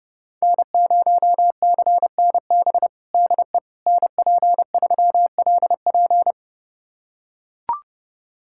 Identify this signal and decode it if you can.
Morse code, heard as a keyed tone, about 31 wpm, 700 Hz. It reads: N0CNB DE NP3LP
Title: USA Callsign DE Callsign - 5 Characters - 30wpm